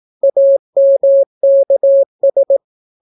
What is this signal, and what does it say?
Morse code, heard as a keyed tone, about 18 wpm, 550 Hz. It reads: AMKS